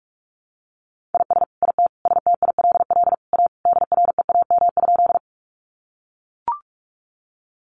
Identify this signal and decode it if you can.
Morse code, heard as a keyed tone, about 44 wpm, 700 Hz. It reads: IS IT STILL A DREAM?
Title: Sentences from Top 1500 Words 45wpm